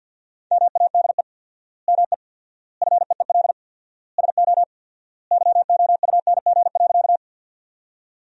Morse code, heard as a keyed tone, about 50 wpm, 700 Hz. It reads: MADE ME FEEL SO YOUNG.
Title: 4x Word-Spacing - Sentences from Top 400 Words - 50wpm